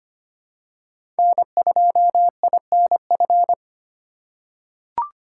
Morse code, heard as a keyed tone, about 25 wpm, 700 Hz. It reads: N2INF
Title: USA Call Signs - 5 Characters - 4x Character Spacing - 25wpm